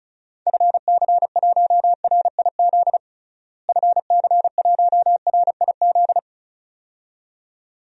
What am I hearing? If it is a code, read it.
Morse code, heard as a keyed tone, about 36 wpm, 700 Hz. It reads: FC1RIZ FC1RIZ
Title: DXCC-100 Call Signs - 6 Characters - Repeated 2x - 35wpm